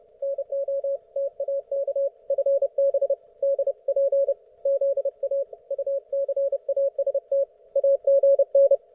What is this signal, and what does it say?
Morse code, heard as a keyed tone, about 30 wpm, 550 Hz. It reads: NOTAK FB DP ZA UCAST AGN
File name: Archiv / CUC181 / ok1fhi cul181